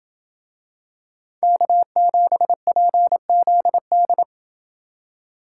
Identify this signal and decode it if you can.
Morse code, heard as a keyed tone, about 26 wpm, 700 Hz. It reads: K7PZD